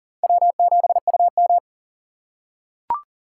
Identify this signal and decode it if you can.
Morse code, heard as a keyed tone, about 40 wpm, 700 Hz. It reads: W7UM
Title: USA Call Signs - 4 Characters - 3x Character Spacing - 40wpm